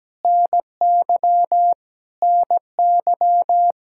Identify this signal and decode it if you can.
Morse code, heard as a keyed tone, about 17 wpm, 700 Hz. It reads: NY NY
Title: Practice QSO - Exchange 1 - Beginning - 17wpm